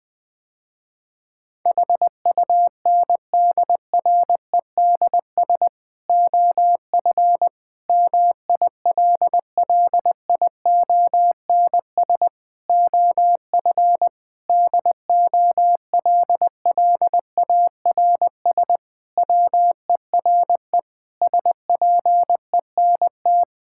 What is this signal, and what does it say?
Morse code, heard as a keyed tone, about 21 wpm, 700 Hz. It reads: HUNDREDS OF MILLIONS OF DOLLARS WERE SPENT
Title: Sentences with 5-Word Phrases - Spoken Beginning - 20wpm